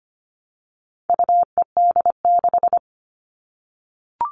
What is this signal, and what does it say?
Morse code, heard as a keyed tone, about 25 wpm, 700 Hz. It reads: UED6
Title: All Cut Numbers - 4-Digit 25wpm